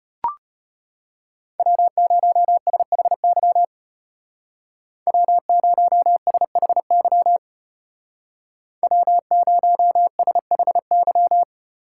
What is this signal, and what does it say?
Morse code, heard as a keyed tone, about 35 wpm, 700 Hz. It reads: W0SHY W0SHY W0SHY